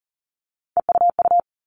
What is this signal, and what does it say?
Morse code, heard as a keyed tone, about 40 wpm, 700 Hz. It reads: EUU
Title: All Cut Numbers - 3-Digit 40wpm